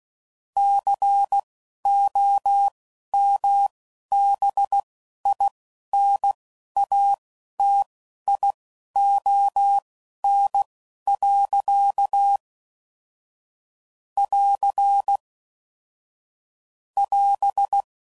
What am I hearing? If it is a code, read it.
Morse code, heard as a keyed tone, about 16 wpm, spaced as for 11 wpm, 800 Hz. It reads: COMBINATION. <AR> <AS>
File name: CW Code Groups @10WPM / Technical Subjects / SwPwrSup